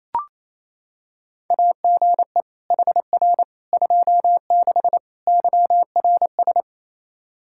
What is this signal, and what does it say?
Morse code, heard as a keyed tone, about 28 wpm, 700 Hz. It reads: AGE HR 26 YRS
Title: Practice QSO - Exchange 3 - More Info - 28wpm